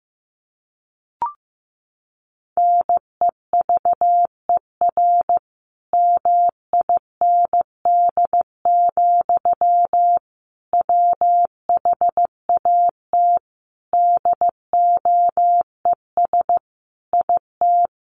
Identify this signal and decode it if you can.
Morse code, heard as a keyed tone, about 15 wpm, 700 Hz. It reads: NEVER MIND, WHAT DOES IT
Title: Sentences from Top 700 Words 15wpm